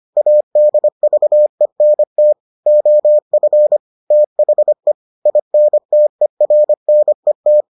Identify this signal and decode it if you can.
Morse code, heard as a keyed tone, about 25 wpm, 600 Hz. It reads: ADVENT OF THE INTERNET